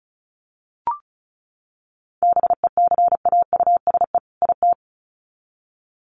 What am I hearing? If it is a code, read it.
Morse code, heard as a keyed tone, about 34 wpm, 700 Hz. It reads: BECAUSE IT
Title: Sets of 2 Words 35wpm